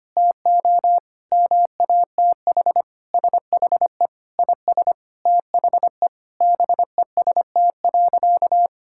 Morse code, heard as a keyed tone, about 25 wpm, 700 Hz. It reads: TO MATH SHE IS THE BEST.
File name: practice / 25wpm / Sentences with 4-Word Phrases - 25wpm